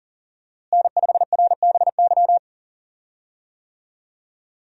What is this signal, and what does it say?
Morse code, heard as a keyed tone, about 40 wpm, 700 Hz. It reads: N5RBY
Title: USA Call Signs - 5 Characters - Repeated 3x - 40wpm